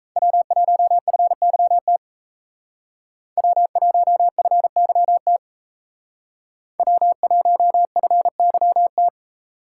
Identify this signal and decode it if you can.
Morse code, heard as a keyed tone, about 35 wpm, 700 Hz. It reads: W1FYT W1FYT W1FYT